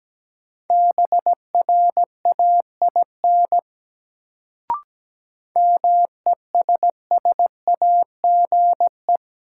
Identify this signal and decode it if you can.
Morse code, heard as a keyed tone, about 17 wpm, 700 Hz. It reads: BRAIN MESSAGE
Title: Sentences from Top 1100 Words - New Word Drill Alt - 17wpm